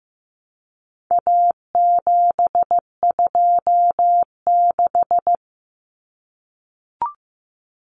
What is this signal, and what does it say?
Morse code, heard as a keyed tone, about 15 wpm, 700 Hz. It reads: A726
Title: Common Cut Numbers - 4-Digit 15wpm